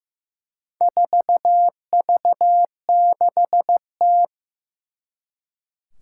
Morse code, heard as a keyed tone, about 15 wpm, 700 Hz. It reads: 4V6T
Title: Four Challenging Letter-Number Combinations 15wpm